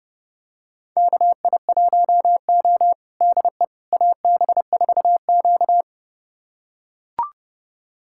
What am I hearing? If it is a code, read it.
Morse code, heard as a keyed tone, about 30 wpm, 700 Hz. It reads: KI1O DE AB4Q
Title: USA Callsign DE Callsign - 4 Characters - 30wpm